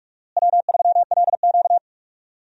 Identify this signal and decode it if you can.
Morse code, heard as a keyed tone, about 46 wpm, 700 Hz. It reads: W3LQ
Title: USA Call Signs - 4 Characters - Repeated 2x - 45wpm